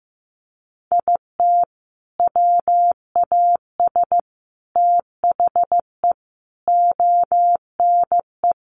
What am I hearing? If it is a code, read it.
Morse code, heard as a keyed tone, about 15 wpm, 700 Hz. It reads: IT WAS THE ONE
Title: Sentences from Top 1400 Words 15wpm